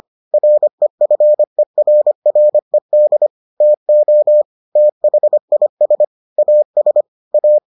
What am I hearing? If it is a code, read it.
Morse code, heard as a keyed tone, about 25 wpm, 600 Hz. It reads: REFERRED TO THIS AS A